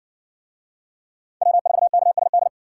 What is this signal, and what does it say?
Morse code, heard as a keyed tone, about 55 wpm, 700 Hz. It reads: W4KSD